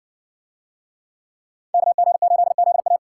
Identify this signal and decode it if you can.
Morse code, heard as a keyed tone, about 56 wpm, 700 Hz. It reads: KG8ZA